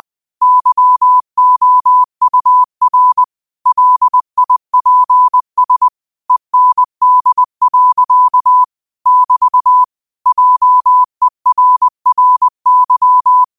Morse code, heard as a keyed tone, about 20 wpm, 1000 Hz. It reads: YOUR LIPS END. <BT> JERRY
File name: qotdfiles / 20wpm